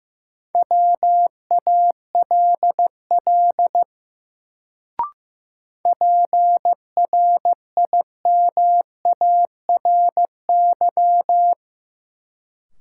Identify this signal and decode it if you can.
Morse code, heard as a keyed tone, about 15 wpm, 700 Hz. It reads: WALL PRIMARY
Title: Sentences from Top 1100 Words - New Word Drill Alt - 15wpm